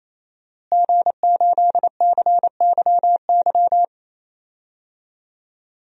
Morse code, heard as a keyed tone, about 27 wpm, 700 Hz. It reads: G8CYY
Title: DXCC-100 Call Signs - 5 Characters - Repeated 3x - 28wpm